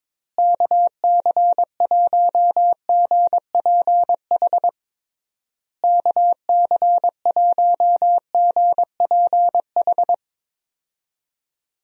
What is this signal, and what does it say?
Morse code, heard as a keyed tone, about 22 wpm, 700 Hz. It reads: KC1GPH KC1GPH
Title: USA Call Signs - 6 Characters - Repeated 3x - 22wpm